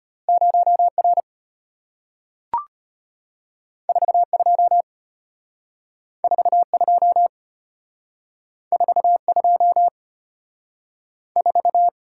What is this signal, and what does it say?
Morse code, heard as a keyed tone, about 34 wpm, 700 Hz. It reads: 0R 42 42 42 4